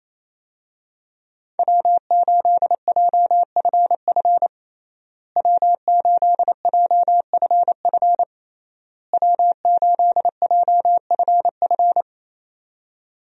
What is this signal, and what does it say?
Morse code, heard as a keyed tone, about 27 wpm, 700 Hz. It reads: W8JFF W8JFF W8JFF
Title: USA Call Signs - 5 Characters - Repeated 3x - 28wpm